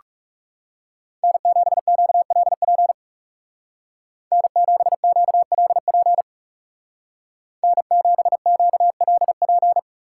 Morse code, heard as a keyed tone, about 37 wpm, 700 Hz. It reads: N7QLP N7QLP N7QLP
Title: Speed Racing - USA 1x3 Call Signs 30wpm